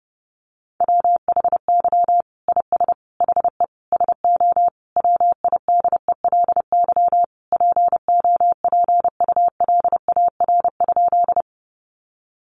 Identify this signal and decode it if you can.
Morse code, heard as a keyed tone, about 30 wpm, 700 Hz. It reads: WHY IS HE SO WIDELY POPULAR?
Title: Sentences from Top 1200 Words 30wpm